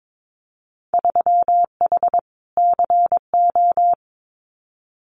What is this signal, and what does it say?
Morse code, heard as a keyed tone, about 22 wpm, 700 Hz. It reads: 3H CO
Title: Winter Field Day Exchanges 22wpm